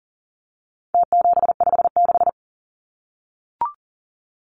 Morse code, heard as a keyed tone, about 40 wpm, 700 Hz. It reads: T756
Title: Numbers - Contest Style - 4-Digit 40wpm